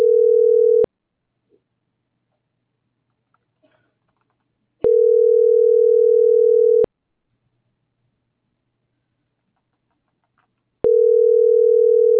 Ringback tone (North America), 440 + 480 Hz.